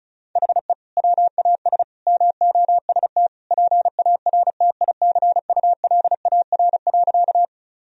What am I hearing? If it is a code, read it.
Morse code, heard as a keyed tone, about 34 wpm, 700 Hz. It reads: HE WAS MOST PARTICULAR.
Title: Sentences from Top 1000 Words - Review Part 4 - 35wpm